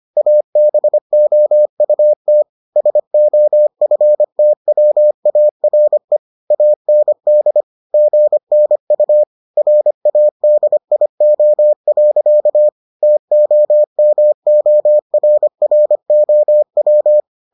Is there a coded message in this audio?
Yes: Morse code from 0.2 to 17.2 s, heard as a keyed tone, about 25 wpm, 600 Hz. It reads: ABOUT SOFTWARE AND GNU RADIO. TOMORROW